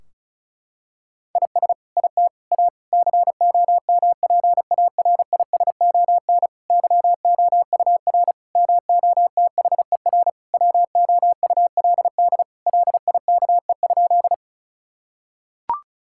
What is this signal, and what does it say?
Morse code, heard as a keyed tone, about 36 wpm, 700 Hz. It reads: IS IT A COMPARISON YOUR MOTHER WOULD LIKE?
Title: Sentences from Top 1600 Words 35wpm